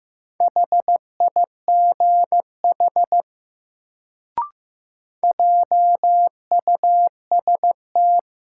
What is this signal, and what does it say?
Morse code, heard as a keyed tone, about 15 wpm, 700 Hz. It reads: HIGH JUST
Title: Sentences from Top 200 Words - New Word Drill - 15wpm